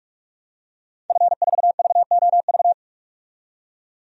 Morse code, heard as a keyed tone, about 46 wpm, 700 Hz. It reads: F4VOV
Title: DXCC-100 Call Signs - 5 Characters - 45wpm